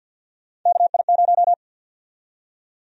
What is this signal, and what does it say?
Morse code, heard as a keyed tone, about 53 wpm, 700 Hz. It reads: KI0
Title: USA Call Sign Prefixes - 50wpm